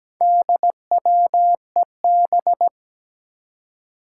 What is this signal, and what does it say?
Morse code, heard as a keyed tone, about 17 wpm, 700 Hz. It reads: DWEB
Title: Four Challenging Letter-Number Combinations 17wpm